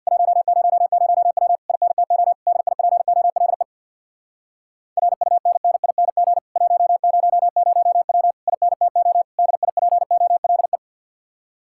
Morse code, heard as a keyed tone, about 63 wpm, 700 Hz. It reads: 100W INTO DIPOLE RUNNING 100W INTO DIPOLE